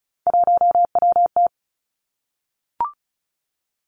Morse code, heard as a keyed tone, about 36 wpm, 700 Hz. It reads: 1WT
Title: Three Challenging Letter-Number Combinations 35wpm